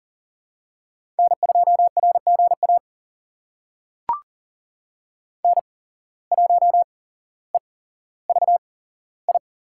Morse code, heard as a keyed tone, about 40 wpm, spaced as for 17 wpm, 700 Hz. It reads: N2RGA N1EVI
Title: USA Call Signs - 5 Characters - 8x Character Spacing - 40wpm